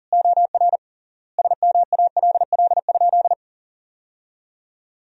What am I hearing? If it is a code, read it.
Morse code, heard as a keyed tone, about 40 wpm, 700 Hz. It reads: OR SMALL?
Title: 3x Word-Spacing - Sentences from Top 200 Words - 40wpm